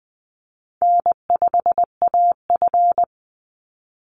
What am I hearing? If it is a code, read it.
Morse code, heard as a keyed tone, about 20 wpm, 700 Hz. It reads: N5AF
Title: Easy Call Signs Repeated 2x - 20wpm